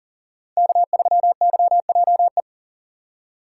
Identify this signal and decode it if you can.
Morse code, heard as a keyed tone, about 40 wpm, 700 Hz. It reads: K3YJE
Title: USA Call Signs - 5 Characters - 3x Character Spacing - 40wpm